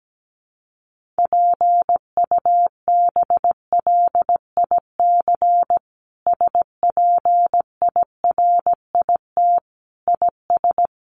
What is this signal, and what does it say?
Morse code, heard as a keyed tone, about 17 wpm, 700 Hz. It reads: PUBLIC SPIRIT IS
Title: Sentences from Top 1400 Words 17wpm